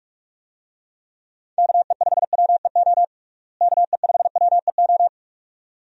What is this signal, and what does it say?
Morse code, heard as a keyed tone, about 47 wpm, 700 Hz. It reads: KE5WEO KE5WEO
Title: USA Call Signs - 6 Characters - Repeated 2x - 45wpm